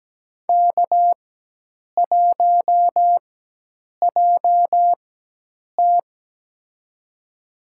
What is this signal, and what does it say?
Morse code, heard as a keyed tone, about 17 wpm, spaced as for 8 wpm, 700 Hz. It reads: K1JT
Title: USA Call Signs - 4 Characters - 4x Character Spacing - 17wpm